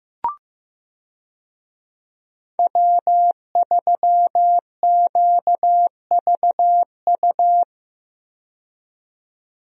Morse code, heard as a keyed tone, about 15 wpm, 700 Hz. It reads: W3QVU
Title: USA Call Signs - 5 Characters - 15wpm